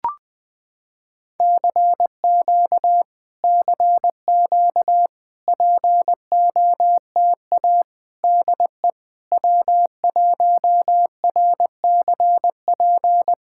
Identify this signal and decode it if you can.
Morse code, heard as a keyed tone, about 20 wpm, 700 Hz. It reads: CQ CQ POTA DE W1RCP
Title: POTA QSO Practice - 20wpm